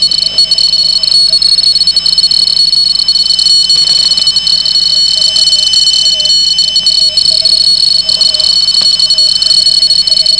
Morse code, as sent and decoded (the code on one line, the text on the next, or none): none